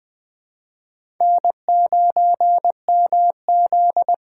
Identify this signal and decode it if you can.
Morse code, heard as a keyed tone, about 20 wpm, 700 Hz. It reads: N9MZ
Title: USA Call Signs - 4 Characters - 6x Character Spacing - 20wpm